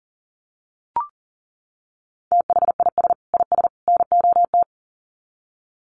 Morse code, heard as a keyed tone, about 38 wpm, 700 Hz. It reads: THIS IS NOT
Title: Sets of 3 Words 40wpm